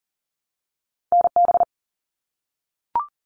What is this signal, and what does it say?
Morse code, heard as a keyed tone, about 38 wpm, 700 Hz. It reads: NB
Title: Canadian Provinces and Territories 40wpm